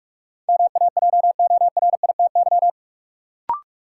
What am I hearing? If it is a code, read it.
Morse code, heard as a keyed tone, about 46 wpm, 700 Hz. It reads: MAJORITY
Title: Sets of 3 Words - New Word Drill - 45wpm